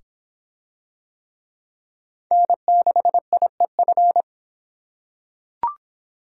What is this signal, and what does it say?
Morse code, heard as a keyed tone, about 26 wpm, 700 Hz. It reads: N6IEF